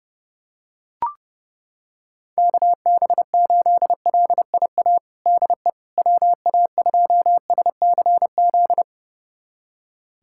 Morse code, heard as a keyed tone, about 30 wpm, 700 Hz. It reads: KB8LIA DE WA2SCZ
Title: USA Callsign DE Callsign - 6 Characters - 30wpm